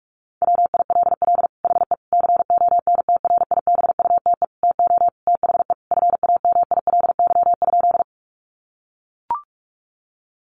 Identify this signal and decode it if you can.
Morse code, heard as a keyed tone, about 44 wpm, 700 Hz. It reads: WILL HE CONTRIBUTE TO THE FAMILY?
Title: Sentences from Top 1400 Words 45wpm